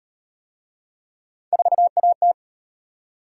Morse code, heard as a keyed tone, about 37 wpm, 700 Hz. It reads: 4AT